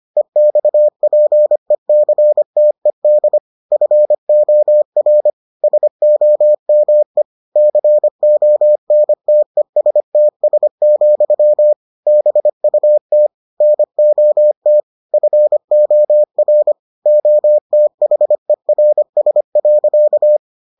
Morse code, heard as a keyed tone, about 25 wpm, 600 Hz. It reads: EXPECTED FOR SOME CONTESTS, BUT NOT FOR OTHERS.